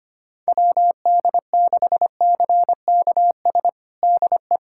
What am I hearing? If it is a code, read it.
Morse code, heard as a keyed tone, about 25 wpm, 700 Hz. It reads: WD6CKS DE
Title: USA Callsign DE Callsign - 6 Characters - 25wpm